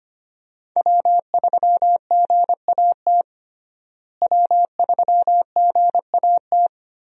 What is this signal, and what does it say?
Morse code, heard as a keyed tone, about 25 wpm, 700 Hz. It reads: W3GAT W3GAT
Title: USA Call Signs - 5 Characters - Repeated 3x - 25wpm